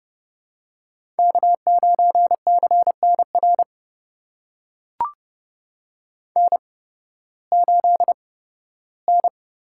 Morse code, heard as a keyed tone, about 32 wpm, spaced as for 16 wpm, 700 Hz. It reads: K9CNR N8N
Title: USA Call Signs - 5 Characters - 8x Character Spacing - 30wpm